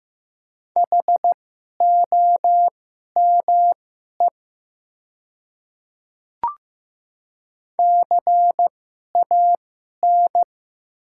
Morse code, heard as a keyed tone, about 15 wpm, spaced as for 11 wpm, 700 Hz. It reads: HOME CAN
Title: 2x Character Spacing - Top 300 Words - 15wpm